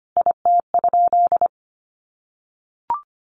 Morse code, heard as a keyed tone, about 25 wpm, 700 Hz. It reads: IT?
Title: Sentences from Top 600 Words 25wpm